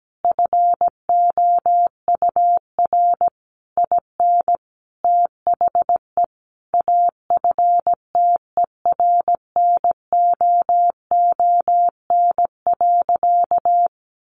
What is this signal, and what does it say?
Morse code, heard as a keyed tone, about 17 wpm, 700 Hz. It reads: FOUR IN THE AFTERNOON.
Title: Sentences from Top 1500 Words 17wpm